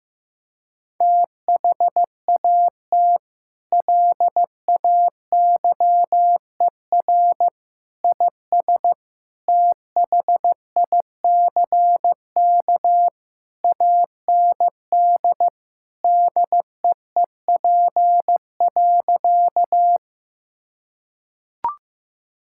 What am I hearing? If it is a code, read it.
Morse code, heard as a keyed tone, about 15 wpm, 700 Hz. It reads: THAT LAYER IS THICK AND DEEP.
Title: Sentences from Top 1600 Words 15wpm